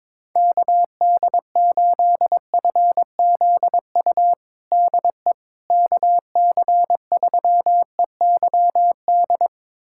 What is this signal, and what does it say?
Morse code, heard as a keyed tone, about 22 wpm, 700 Hz. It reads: KD8FZU DE KC3EYD
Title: USA Callsign DE Callsign 22wpm